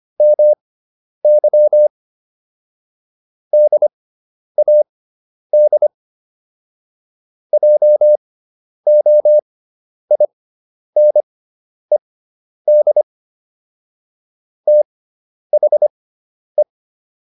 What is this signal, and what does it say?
Morse code, heard as a keyed tone, about 25 wpm, spaced as for 10 wpm, 600 Hz. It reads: MY DAD JOINED THE